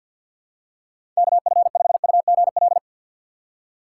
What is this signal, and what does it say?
Morse code, heard as a keyed tone, about 52 wpm, 700 Hz. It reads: KF5UGL
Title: USA Call Signs - 6 Characters - 4x Character Spacing - 50wpm